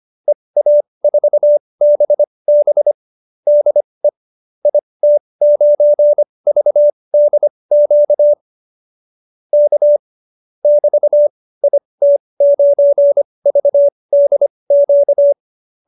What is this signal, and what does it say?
Morse code, heard as a keyed tone, about 25 wpm, spaced as for 20 wpm, 590 Hz. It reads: EA4BB DE IT9VDQ K <BT> IT9VDQ
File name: qso215b 20